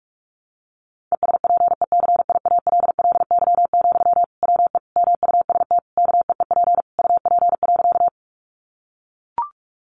Morse code, heard as a keyed tone, about 44 wpm, 700 Hz. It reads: ESPECIALLY, WE MUST KEEP UP.
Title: Sentences from Top 700 Words 45wpm